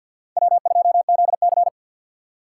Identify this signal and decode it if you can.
Morse code, heard as a keyed tone, about 52 wpm, 700 Hz. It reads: W2ZC
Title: USA Call Signs - 4 Characters - 3x Character Spacing - 50wpm